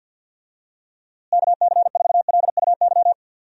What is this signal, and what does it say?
Morse code, heard as a keyed tone, about 51 wpm, 700 Hz. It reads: KC4LUY
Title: USA Call Signs - 6 Characters - Repeated 3x - 50wpm